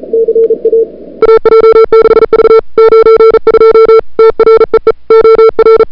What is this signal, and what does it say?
Morse code, heard as a keyed tone, about 34 wpm, 440 Hz. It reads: CA AJ6V 92 TREE OR
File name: n6tr / Fab2007CWSprintHour1